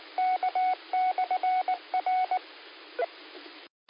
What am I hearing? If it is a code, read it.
Morse code, heard as a keyed tone, about 19 wpm, 740 Hz. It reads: K/R
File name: qsorec SimplexLogic 13-01-55